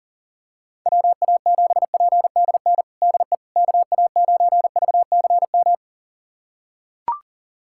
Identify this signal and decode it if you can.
Morse code, heard as a keyed tone, about 40 wpm, 700 Hz. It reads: WA7PDN DE KA9VCM
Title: USA Callsign DE Callsign 40wpm